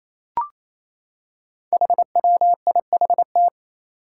Morse code, heard as a keyed tone, about 27 wpm, 700 Hz. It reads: HWIHT
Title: Five Challenging Letter-Number Combinations 28wpm